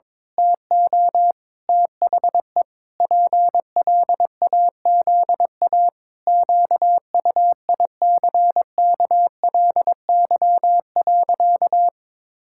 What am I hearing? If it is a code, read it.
Morse code, heard as a keyed tone, about 22 wpm, 700 Hz. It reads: TO THE PLAZA QUICKLY.
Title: Pangram Challenge v2 - 22wpm